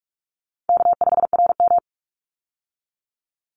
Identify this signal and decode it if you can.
Morse code, heard as a keyed tone, about 44 wpm, 700 Hz. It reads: K5RM
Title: Easy Call Signs - 45wpm